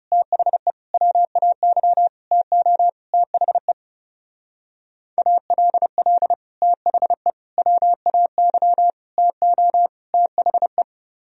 Morse code, heard as a keyed tone, about 35 wpm, 700 Hz. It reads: THE WAY TO THE ALL THE WAY TO THE
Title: Speed-Racing - Sets of 5 Words 30wpm